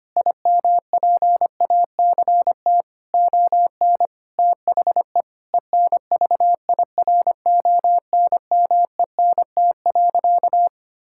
Morse code, heard as a keyed tone, about 25 wpm, 700 Hz. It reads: IMPACT ON THE ENVIRONMENT.
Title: Sentences from Top 1000 Words - Review Part 6 - 25wpm